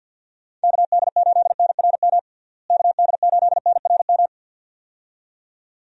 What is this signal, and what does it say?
Morse code, heard as a keyed tone, about 47 wpm, 700 Hz. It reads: KD8NRM KD8NRM
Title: USA Call Signs - 6 Characters - Repeated 2x - 50wpm